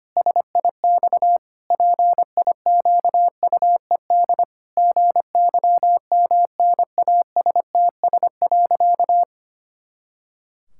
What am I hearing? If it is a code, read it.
Morse code, heard as a keyed tone, about 25 wpm, 700 Hz. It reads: SIX PIQUED GYMNASTS.
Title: Pangram Challenge v2 - 25wpm